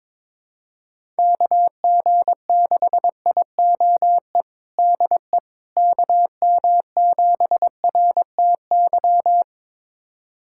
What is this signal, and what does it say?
Morse code, heard as a keyed tone, about 22 wpm, 700 Hz. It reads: KG6IOE DE KM7RTY
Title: USA Callsign DE Callsign 22wpm